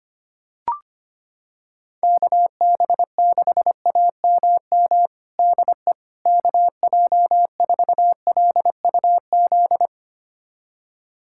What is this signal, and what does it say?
Morse code, heard as a keyed tone, about 25 wpm, 700 Hz. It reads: KB6AMM DE KJ4LUZ
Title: USA Callsign DE Callsign - 6 Characters - 25wpm